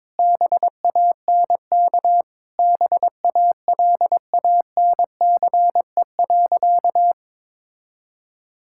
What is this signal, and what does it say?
Morse code, heard as a keyed tone, about 21 wpm, 700 Hz. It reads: BANK BALANCE.
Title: Sentences from Top 1000 Words - Review Part 6 - 22wpm